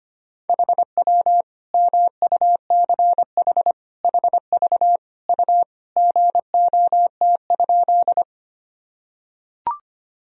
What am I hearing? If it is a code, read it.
Morse code, heard as a keyed tone, about 25 wpm, 700 Hz. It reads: HW MUCH HV U GOT?
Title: Sentences from Top 300 Words and Abbreviations 25wpm